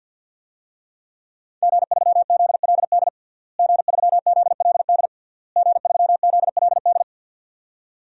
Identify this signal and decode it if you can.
Morse code, heard as a keyed tone, about 52 wpm, 700 Hz. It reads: G3ZLD G3ZLD G3ZLD
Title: DXCC-100 Call Signs - 5 Characters - Repeated 3x - 50wpm